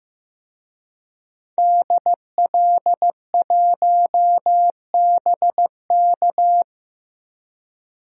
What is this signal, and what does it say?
Morse code, heard as a keyed tone, about 15 wpm, 700 Hz. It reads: DL1BK
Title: DXCC-100 Call Signs - 5 Characters - 15wpm